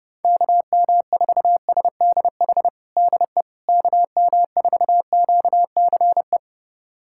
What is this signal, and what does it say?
Morse code, heard as a keyed tone, about 30 wpm, 700 Hz. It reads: KM4SDH DE KM4QCE
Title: USA Callsign DE Callsign 30wpm